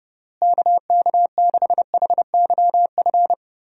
Morse code, heard as a keyed tone, about 30 wpm, 700 Hz. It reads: KK6HYF
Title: USA Call Signs - 6 Characters - Repeated 3x - 30wpm